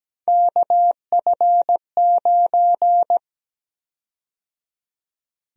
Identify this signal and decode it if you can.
Morse code, heard as a keyed tone, about 17 wpm, 700 Hz. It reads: KF9
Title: USA Call Sign Prefixes - 17wpm